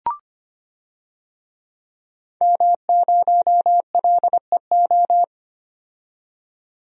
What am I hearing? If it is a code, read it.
Morse code, heard as a keyed tone, about 25 wpm, 700 Hz. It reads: M0LEO
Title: DXCC-100 Call Signs - 5 Characters - 25wpm